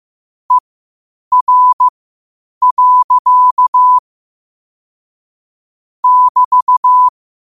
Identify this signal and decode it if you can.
Morse code, heard as a keyed tone, about 15 wpm, spaced as for 8 wpm, 1000 Hz. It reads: ER. <BT>
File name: qotdfiles / 8wpm